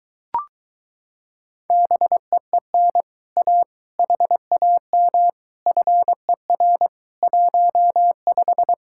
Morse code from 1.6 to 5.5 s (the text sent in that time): BEEN A HAM